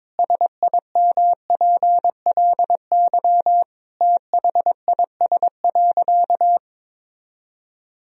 Morse code, heard as a keyed tone, about 22 wpm, 700 Hz. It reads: SIMPLY THIS.
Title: Sentences from Top 1000 Words - Review Part 4 - 22wpm